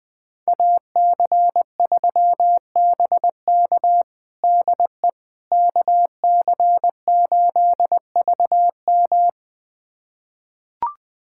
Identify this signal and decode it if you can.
Morse code, heard as a keyed tone, about 20 wpm, 700 Hz. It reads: AC3BK DE KC8VM
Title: USA Callsign DE Callsign - 5 Characters - 20wpm